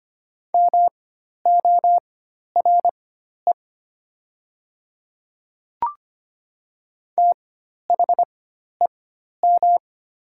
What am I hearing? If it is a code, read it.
Morse code, heard as a keyed tone, about 25 wpm, spaced as for 12 wpm, 700 Hz. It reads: MORE THEM
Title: 4x Character Spacing - Top 500 Words - 25wpm